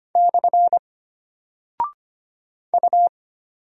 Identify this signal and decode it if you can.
Morse code, heard as a keyed tone, about 25 wpm, 700 Hz. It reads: / U
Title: Single Character (Letter:Number:Punctuation:Prosign) Alt 25wpm